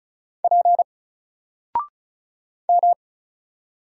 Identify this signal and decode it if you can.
Morse code, heard as a keyed tone, about 34 wpm, 700 Hz. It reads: P M
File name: Single-Letter-Number-35wpm